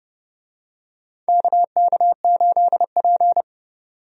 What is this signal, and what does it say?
Morse code, heard as a keyed tone, about 30 wpm, 700 Hz. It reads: KK8P
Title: USA Call Signs - 4 Characters - 30wpm